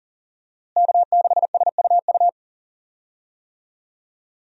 Morse code, heard as a keyed tone, about 40 wpm, 700 Hz. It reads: K6SUU
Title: USA Call Signs - 5 Characters - Repeated 3x - 40wpm